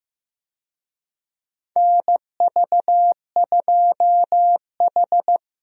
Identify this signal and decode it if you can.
Morse code, heard as a keyed tone, about 16 wpm, 700 Hz. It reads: NV2H
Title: USA Call Signs - 4 Characters - Repeated 2x - 15wpm